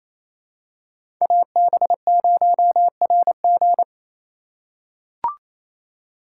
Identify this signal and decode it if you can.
Morse code, heard as a keyed tone, about 27 wpm, 700 Hz. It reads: AB0RG
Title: USA Call Signs - 5 Characters - 2x Character Spacing - 28wpm